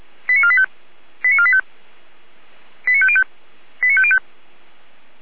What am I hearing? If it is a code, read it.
Five-tone selective call (ZVEI), heard as quick runs of five tones, tones of about 70 ms. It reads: 89485 89485 89505 89505